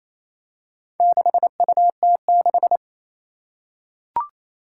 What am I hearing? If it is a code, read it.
Morse code, heard as a keyed tone, about 28 wpm, 700 Hz. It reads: 6UT6
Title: Four Challenging Letter-Number Combinations 28wpm